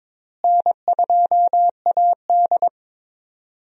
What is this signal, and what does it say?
Morse code, heard as a keyed tone, about 22 wpm, 700 Hz. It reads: N2AD
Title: USA Call Signs - 4 Characters - Repeated 3x - 22wpm